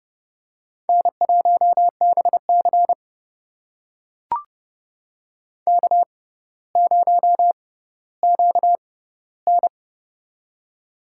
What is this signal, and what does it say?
Morse code, heard as a keyed tone, about 30 wpm, spaced as for 15 wpm, 700 Hz. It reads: N1BC K0QN
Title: USA Call Signs - 4 Characters - 6x Character Spacing - 30wpm